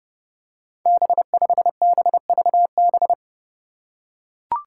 Morse code, heard as a keyed tone, about 32 wpm, 700 Hz. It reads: B5BVB
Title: Five Challenging Letter-Number Combinations 30wpm